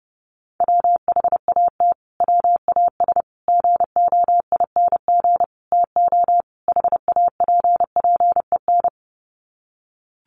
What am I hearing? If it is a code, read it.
Morse code, heard as a keyed tone, about 30 wpm, 700 Hz. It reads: WHAT WAS GOING TO HAPPEN
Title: Sets of 5 Words Encore 30wpm